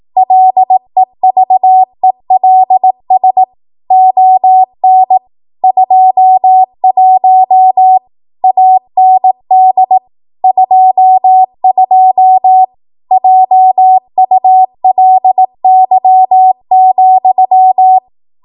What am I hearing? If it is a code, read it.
Morse code, heard as a keyed tone, about 18 wpm, 750 Hz. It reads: LEVELS ON 21 AND 22 JULY,
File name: Morse / 18WPM